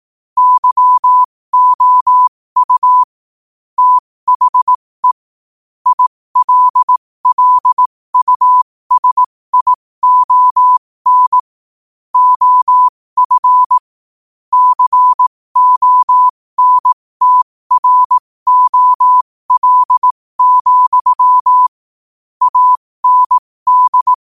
Morse code, heard as a keyed tone, about 18 wpm, spaced as for 15 wpm, 1000 Hz. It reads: YOU THE ILLUSION OF CONTROL, AND
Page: qotd-at-15-wpm